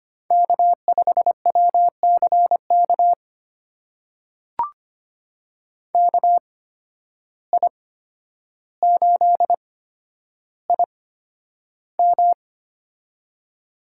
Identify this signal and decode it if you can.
Morse code, heard as a keyed tone, about 25 wpm, spaced as for 11 wpm, 700 Hz. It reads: K5WCK KI8IM
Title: USA Call Signs - 5 Characters - 8x Character Spacing - 25wpm